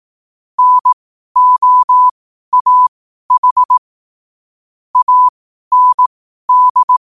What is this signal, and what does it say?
Morse code, heard as a keyed tone, about 18 wpm, spaced as for 12 wpm, 1000 Hz. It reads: NOAH AND
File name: 12wpm